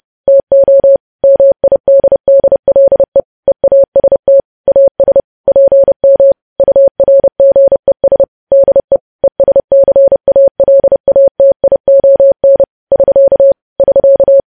Morse code, heard as a keyed tone, about 30 wpm, 550 Hz. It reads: TO MIDDLE EAST AS PM URGES DE ESCALATION <SK> <SK>
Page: news headlines in morse code at 30 wpm